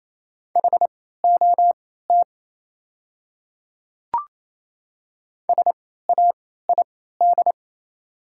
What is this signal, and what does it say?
Morse code, heard as a keyed tone, about 27 wpm, spaced as for 16 wpm, 700 Hz. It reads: HOT SAID
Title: 3x Character Spacing - Top 100 Words - 28wpm